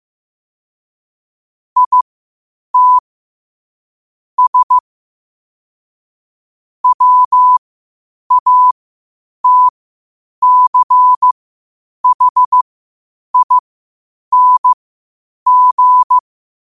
Morse code, heard as a keyed tone, about 15 wpm, spaced as for 8 wpm, 1000 Hz. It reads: IT S WATCHING
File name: qotdfiles / 8wpm